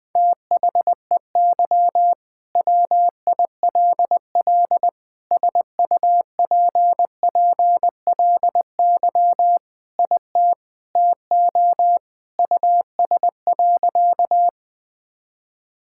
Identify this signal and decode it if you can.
Morse code, heard as a keyed tone, about 20 wpm, 700 Hz. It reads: THEY WILL SUPPLY IT TO US.
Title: Sentences from Top 1000 Words - Review Part 5 - 20wpm